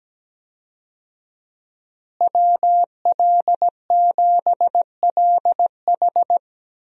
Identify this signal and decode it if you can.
Morse code, heard as a keyed tone, about 17 wpm, 700 Hz. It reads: WL7LH
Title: USA Call Signs - 5 Characters - Repeated 2x - 17wpm